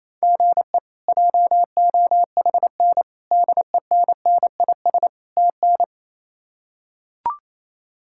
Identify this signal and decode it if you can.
Morse code, heard as a keyed tone, about 28 wpm, 700 Hz. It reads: GE JOHN DENNIS TN
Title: K1USN SST QSO Practice - 28wpm